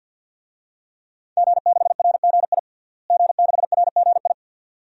Morse code, heard as a keyed tone, about 50 wpm, 700 Hz. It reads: G6RGI G6RGI
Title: DXCC-100 Call Signs - 5 Characters - Repeated 2x - 50wpm